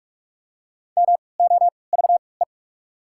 Morse code, heard as a keyed tone, about 46 wpm, 690 Hz. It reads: M O V E
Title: 3x Character Spacing - Top 200 Words - 45wpm